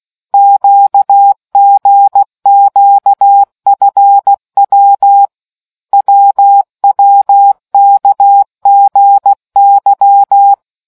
Morse code, heard as a keyed tone, about 16 wpm, 800 Hz. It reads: QGQFW WWKGY